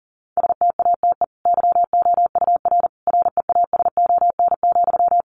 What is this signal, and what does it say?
Morse code, heard as a keyed tone, about 40 wpm, 700 Hz. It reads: STATE YOUR REASON,
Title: Sentences from Top 700 Words 40wpm